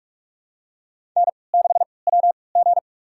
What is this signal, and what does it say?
Morse code, heard as a keyed tone, about 44 wpm, spaced as for 25 wpm, 700 Hz. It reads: N6WG
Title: USA Call Signs - 4 Characters - 3x Character Spacing - 45wpm